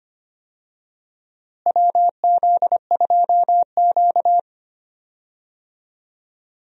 Morse code, heard as a keyed tone, about 25 wpm, 700 Hz. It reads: WZ2Q
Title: USA Call Signs - 4 Characters - Repeated 2x - 25wpm